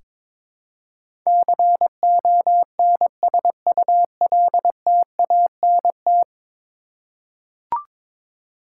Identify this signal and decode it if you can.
Morse code, heard as a keyed tone, about 22 wpm, 700 Hz. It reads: CONSULTANT
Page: morse-code-ninja-practice-22wpm